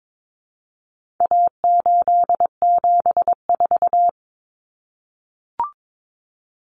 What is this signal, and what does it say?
Morse code, heard as a keyed tone, about 22 wpm, 700 Hz. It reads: A874
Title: Common Cut Numbers - 4-Digit 22wpm